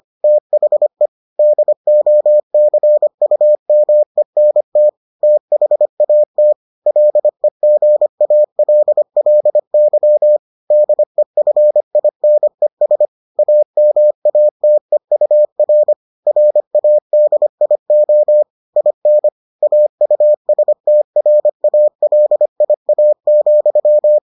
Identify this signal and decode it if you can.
Morse code, heard as a keyed tone, about 25 wpm, 600 Hz. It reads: THE DOCUMENT THAT LEGALLY DEFINES AMATEUR RADIO IN AUSTRALIA,